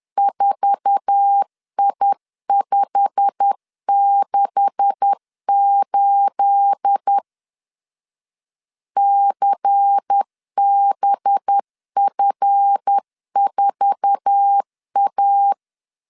Morse code, heard as a keyed tone, about 10 wpm, 800 Hz. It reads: 4I568 CBF4A
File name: LEZIONI / 9 286 10wpm 2735